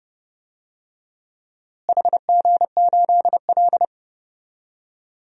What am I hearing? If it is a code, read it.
Morse code, heard as a keyed tone, about 32 wpm, 700 Hz. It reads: HG8L
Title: DXCC-100 Call Signs - 4 Characters - Repeated 3x - 30wpm